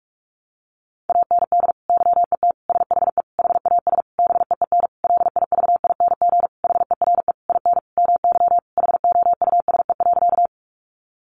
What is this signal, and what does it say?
Morse code, heard as a keyed tone, about 44 wpm, 700 Hz. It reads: AND YET SHE HAS BEEN LIVING HERE IN MY HOUSE.
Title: Sentences from Top 1100 Words 45wpm